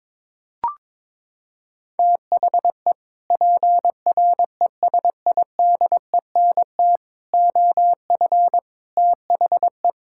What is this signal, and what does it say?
Morse code, heard as a keyed tone, about 21 wpm, 700 Hz. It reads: THE PRESIDENT OF THE
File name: practice / 22wpm / Sets of 8 Words - 22wpm